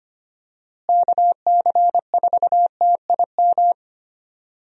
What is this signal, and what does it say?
Morse code, heard as a keyed tone, about 25 wpm, 700 Hz. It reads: KC4TIM
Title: USA Call Signs - 6 Characters - Repeated 3x - 25wpm